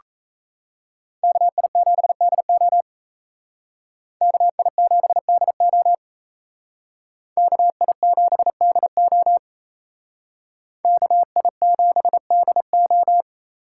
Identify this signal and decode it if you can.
Morse code, heard as a keyed tone, about 36 wpm, 700 Hz. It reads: KI7DO KI7DO KI7DO KI7DO